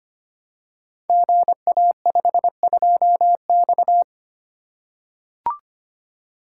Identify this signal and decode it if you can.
Morse code, heard as a keyed tone, about 25 wpm, 700 Hz. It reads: GA52X
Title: Code Groups - 5 Letter:Numbers - 25wpm